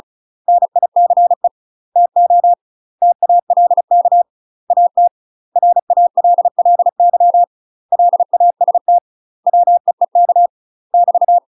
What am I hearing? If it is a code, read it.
Morse code, heard as a keyed tone, about 34 wpm, 700 Hz. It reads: NICE TO TALK AT RALLY LAST WEEK <BT>
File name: UK MASTER QSOs TRAINING / 35WPM / UK MASTER QSOs 0037